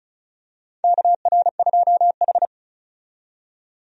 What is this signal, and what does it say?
Morse code, heard as a keyed tone, about 36 wpm, 700 Hz. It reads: KR2H
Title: USA Call Signs - 4 Characters - 35wpm